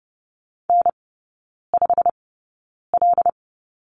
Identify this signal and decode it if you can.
Morse code, heard as a keyed tone, about 30 wpm, spaced as for 9 wpm, 700 Hz. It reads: N5L
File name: course / 30:10wpm / 087 Lesson 15e - Number 5 - Three Characters 30wpm